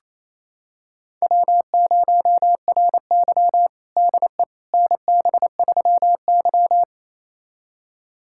Morse code, heard as a keyed tone, about 28 wpm, 700 Hz. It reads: W0RY DE NB3Y
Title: USA Callsign DE Callsign - 4 Characters - 28wpm